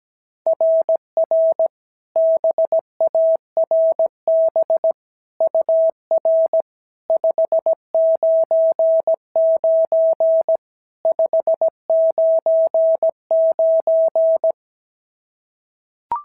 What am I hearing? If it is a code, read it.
Morse code, heard as a keyed tone, about 16 wpm, 650 Hz. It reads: RR BARB UR 599 599
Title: SOTA QSO Practice 17wpm